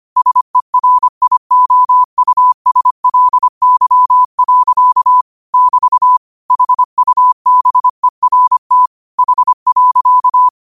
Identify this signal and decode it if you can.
Morse code, heard as a keyed tone, about 25 wpm, 1000 Hz. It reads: SERIOUSLY. <BT> HUBERT H.
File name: qotdfiles / 25wpm